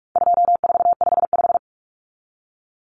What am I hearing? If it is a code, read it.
Morse code, heard as a keyed tone, about 44 wpm, 700 Hz. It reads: 2455
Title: Common Cut Numbers - 4-Digit 45wpm